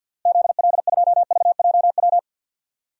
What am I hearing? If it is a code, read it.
Morse code, heard as a keyed tone, about 49 wpm, 700 Hz. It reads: ZL2VJW
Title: DXCC-100 Call Signs - 6 Characters - Repeated 3x - 50wpm